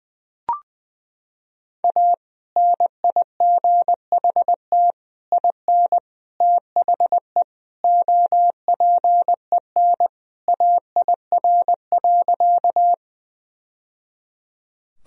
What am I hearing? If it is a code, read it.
Morse code, heard as a keyed tone, about 21 wpm, 700 Hz. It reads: A NIGHT IN THE OPEN AIR.
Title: Sentences from Top 1000 Words - Review Part 1 - 20wpm